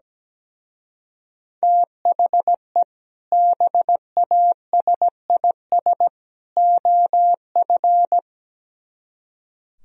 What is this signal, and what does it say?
Morse code, heard as a keyed tone, about 17 wpm, 700 Hz. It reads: THE BASIS OF
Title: Speed-Racing - Sets of 3 Words 17wpm